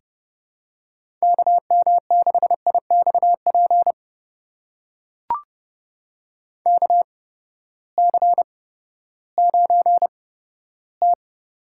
Morse code, heard as a keyed tone, about 30 wpm, spaced as for 15 wpm, 700 Hz. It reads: KM6IXP KC9T
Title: USA Call Signs - 6 Characters - 8x Character Spacing - 30wpm